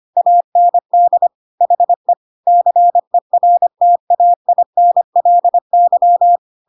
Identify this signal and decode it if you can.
Morse code, heard as a keyed tone, about 25 wpm, 700 Hz. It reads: AND HE CERTAINLY
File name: Pinocchio / Pinocchio ch5 25 25 ik1hge